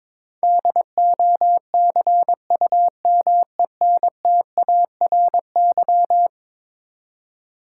Morse code, heard as a keyed tone, about 21 wpm, 700 Hz. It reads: DOCUMENTARY
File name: practice / 22wpm / Eleven Letter Words - 22wpm